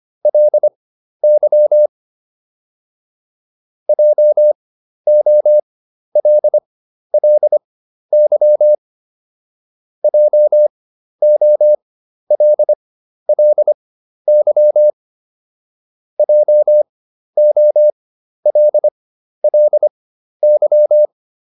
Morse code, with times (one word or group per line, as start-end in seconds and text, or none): 0.3-1.9 LY
3.9-8.7 JOLLY
10.0-14.9 JOLLY
16.2-21.1 JOLLY